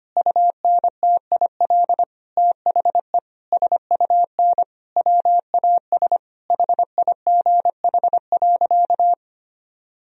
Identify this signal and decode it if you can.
Morse code, heard as a keyed tone, about 25 wpm, 700 Hz. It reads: UNTIL THE SUN WAS HIGH.
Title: Sentences from Top 1000 Words - Review Part 2 - 25wpm